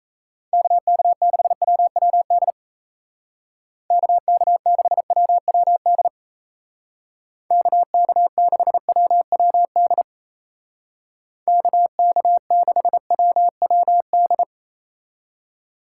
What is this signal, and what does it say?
Morse code, heard as a keyed tone, about 33 wpm, 700 Hz. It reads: KK6WWD KK6WWD KK6WWD KK6WWD
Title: Speed Racing - USA 2x3 Call Signs 28wpm